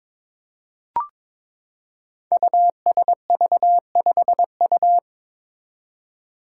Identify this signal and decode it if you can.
Morse code, heard as a keyed tone, about 22 wpm, 700 Hz. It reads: USV5U
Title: Five Challenging Letter-Number Combinations 22wpm